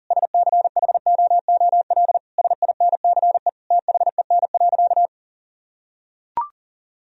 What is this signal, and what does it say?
Morse code, heard as a keyed tone, about 37 wpm, 700 Hz. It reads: SCHOOL SINCE THEN.
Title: Sentences from Top 1000 Words - Review Part 1 - 40wpm